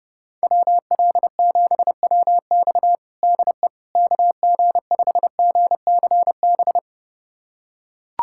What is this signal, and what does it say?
Morse code, heard as a keyed tone, about 30 wpm, 700 Hz. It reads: WL7WX DE KG5GCB
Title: USA Callsign DE Callsign 30wpm